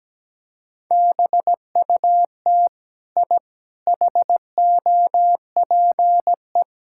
Morse code, heard as a keyed tone, about 17 wpm, 700 Hz. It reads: BUT I HOPE
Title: Sentences from Top 1000 Words - Review Part 6 - 17wpm